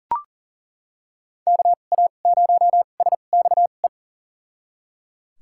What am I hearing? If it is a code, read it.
Morse code, heard as a keyed tone, about 37 wpm, spaced as for 28 wpm, 700 Hz. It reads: KA0SXE
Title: USA Call Signs - 6 Characters - 2x Character Spacing - 40wpm